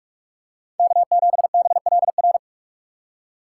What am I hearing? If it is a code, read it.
Morse code, heard as a keyed tone, about 46 wpm, 700 Hz. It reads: K7BLR
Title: USA Call Signs - 5 Characters - 6x Character Spacing - 45wpm